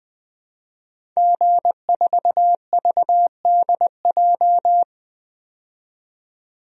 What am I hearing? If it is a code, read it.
Morse code, heard as a keyed tone, about 20 wpm, 700 Hz. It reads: G4VDJ
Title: DXCC-100 Call Signs - 5 Characters - 20wpm